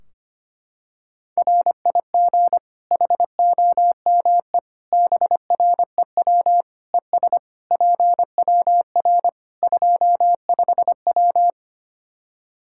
Morse code, heard as a keyed tone, about 25 wpm, 700 Hz. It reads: RIG HOME BREW ES PWR 25W
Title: Practice QSO - Exchange 2 - Info - 25wpm